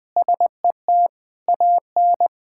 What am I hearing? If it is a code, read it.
Morse code, heard as a keyed tone, about 20 wpm, 700 Hz. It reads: SET AN
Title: Sentences with 8-Word Phrases - 20wpm